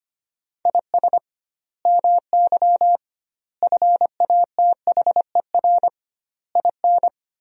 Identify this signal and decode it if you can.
Morse code, heard as a keyed tone, about 25 wpm, 700 Hz. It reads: IS MY FATHER IN
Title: 2x Word-Spacing - Sentences from Top 500 Words - 25wpm